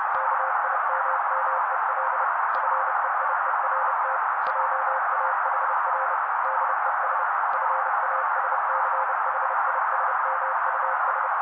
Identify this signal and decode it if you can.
Morse code, heard as a keyed tone, about 30 wpm, 530 Hz. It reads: KIMMEL FEIERT JAHR DES FASCHISMUS
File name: daily / 29wpm noise